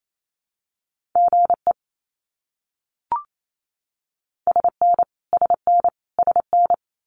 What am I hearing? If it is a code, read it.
Morse code, heard as a keyed tone, about 28 wpm, 700 Hz. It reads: GE SN SN SN
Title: Abbreviations Repeated 3x - 28wpm